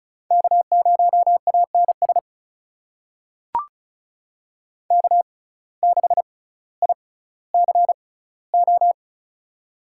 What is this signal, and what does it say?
Morse code, heard as a keyed tone, about 36 wpm, spaced as for 18 wpm, 700 Hz. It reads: K0ANS K6ICO
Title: USA Call Signs - 5 Characters - 6x Character Spacing - 35wpm